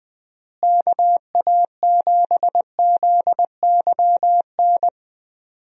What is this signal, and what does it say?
Morse code, heard as a keyed tone, about 20 wpm, 700 Hz. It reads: KA7ZYN
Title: USA Call Signs - 6 Characters - Repeated 2x - 20wpm